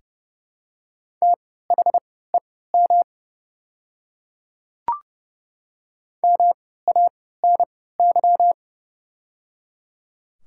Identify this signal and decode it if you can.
Morse code, heard as a keyed tone, about 30 wpm, spaced as for 17 wpm, 700 Hz. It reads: THEM MANY
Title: 3x Character Spacing - Top 100 Words - 30wpm